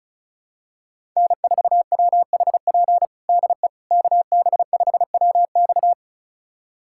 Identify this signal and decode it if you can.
Morse code, heard as a keyed tone, about 36 wpm, 700 Hz. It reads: N4WHP DE KB5WX
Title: USA Callsign DE Callsign - 5 Characters - 35wpm